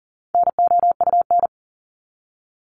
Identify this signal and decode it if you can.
Morse code, heard as a keyed tone, about 40 wpm, 700 Hz. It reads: NOUN
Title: Top 500 Words 40wpm